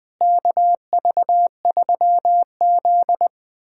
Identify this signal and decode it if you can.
Morse code, heard as a keyed tone, about 20 wpm, 700 Hz. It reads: KV3Z
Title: USA Callsign DE Callsign - 4 Characters - 20wpm